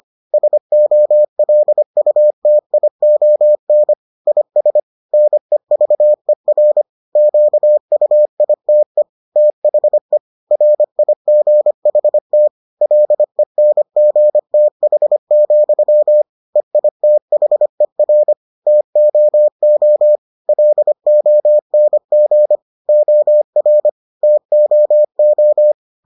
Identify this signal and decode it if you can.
Morse code, heard as a keyed tone, about 25 wpm, 600 Hz. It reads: SOLUTION IS NEVER QUITE THE RIGHT LENGTH, EITHER TOO LONG OR TOO